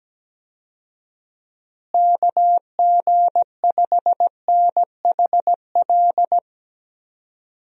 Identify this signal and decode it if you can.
Morse code, heard as a keyed tone, about 17 wpm, 700 Hz. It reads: KG5NHL
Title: USA Call Signs - 6 Characters - Repeated 2x - 17wpm